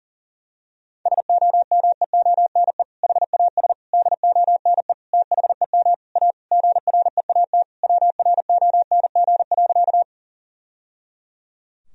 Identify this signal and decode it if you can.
Morse code, heard as a keyed tone, about 40 wpm, 700 Hz. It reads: SOMEONE HAS DONE THEM A GREAT WRONG.
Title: Sentences from Top 1000 Words - Review Part 4 - 40wpm